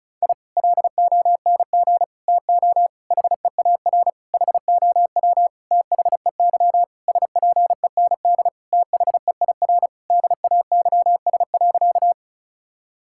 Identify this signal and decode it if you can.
Morse code, heard as a keyed tone, about 36 wpm, 700 Hz. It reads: I LONG TO HEAR HOW THEY SPEND THEIR DAYS.
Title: Sentences from Top 1000 Words - Review Part 5 - 35wpm